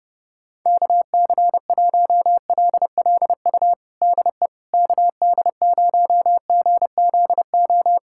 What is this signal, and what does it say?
Morse code, heard as a keyed tone, about 30 wpm, 700 Hz. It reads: KC1LLU DE KD0GZO
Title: USA Callsign DE Callsign 30wpm